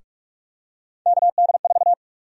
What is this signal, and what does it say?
Morse code, heard as a keyed tone, about 48 wpm, 700 Hz. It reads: KD4
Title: USA Call Sign Prefixes - 45wpm